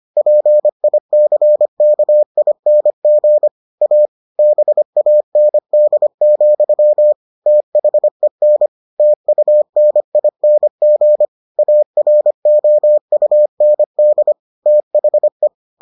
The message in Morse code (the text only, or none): PICKING A BAND, THEN TUNING AROUND THE